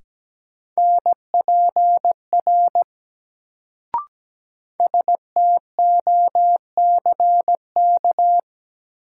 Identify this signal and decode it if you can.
Morse code, heard as a keyed tone, about 17 wpm, 700 Hz. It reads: NPR STOCK
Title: Sets of 5 Words - New Word Drill - 17wpm